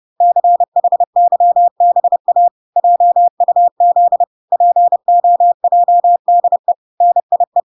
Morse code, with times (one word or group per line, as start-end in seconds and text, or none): 0.2-2.5 CHYBA
2.8-4.2 JUZ
4.5-6.7 POJDE
7.0-7.6 NIE